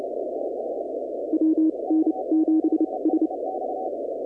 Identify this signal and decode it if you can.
Morse code, heard as a keyed tone, about 29 wpm, spaced as for 23 wpm, 320 Hz. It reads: WN7S
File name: contests / 2019-cw-wpx-kt5j-k5pi- -2330 0-20